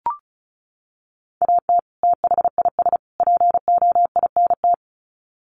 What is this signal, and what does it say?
Morse code, heard as a keyed tone, about 34 wpm, 700 Hz. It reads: AT THIS POINT
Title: Sets of 3 Words 35wpm